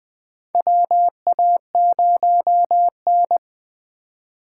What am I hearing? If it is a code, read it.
Morse code, heard as a keyed tone, about 20 wpm, 700 Hz. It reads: WA0N
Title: USA Call Signs - 4 Characters - 3x Character Spacing - 20wpm